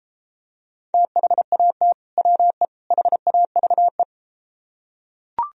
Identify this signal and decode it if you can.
Morse code, heard as a keyed tone, about 32 wpm, 700 Hz. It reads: THAT WE HAVE